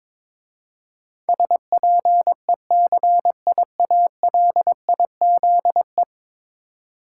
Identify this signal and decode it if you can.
Morse code, heard as a keyed tone, about 22 wpm, 700 Hz. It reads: SPECIALIZE
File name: practice / 22wpm / 10 Letter Words - First 3 Letters Given - 22wpm